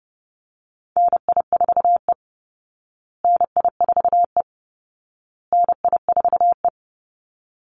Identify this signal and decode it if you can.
Morse code, heard as a keyed tone, about 30 wpm, 700 Hz. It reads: NI4E NI4E NI4E
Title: Call Signs Repeated 2x - 30wpm